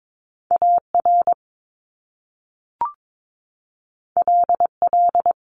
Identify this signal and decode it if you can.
Morse code, heard as a keyed tone, about 22 wpm, 700 Hz. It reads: AR LL
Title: Top Two Letter Combinations in Words 22wpm